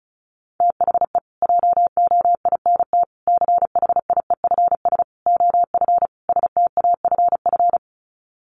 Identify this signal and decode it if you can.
Morse code, heard as a keyed tone, about 34 wpm, 700 Hz. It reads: THE JOINT CHIEFS OF STAFF
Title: Sets of 5 Words Encore2 35wpm